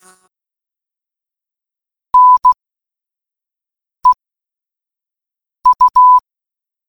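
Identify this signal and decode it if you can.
Morse code, heard as a keyed tone, about 16 wpm, 1000 Hz. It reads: N E U
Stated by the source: Morse code puzzle in an escape game: three coded words giving a padlock's three digits.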